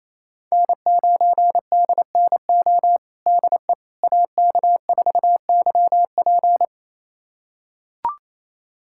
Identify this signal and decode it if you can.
Morse code, heard as a keyed tone, about 27 wpm, 700 Hz. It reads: N9DNO DE AK4YP
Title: USA Callsign DE Callsign - 5 Characters - 28wpm